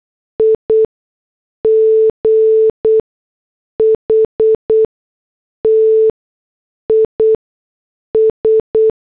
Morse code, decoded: IGHT IS